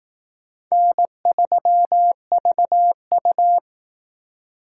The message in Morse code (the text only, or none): N3VU